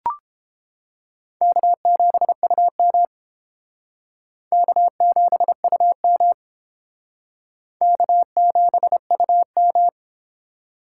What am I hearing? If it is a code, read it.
Morse code, heard as a keyed tone, about 30 wpm, 700 Hz. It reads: K7UM K7UM K7UM